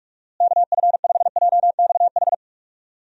Morse code, heard as a keyed tone, about 43 wpm, 700 Hz. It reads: KF5JXH
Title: USA Call Signs - 6 Characters - Repeated 2x - 45wpm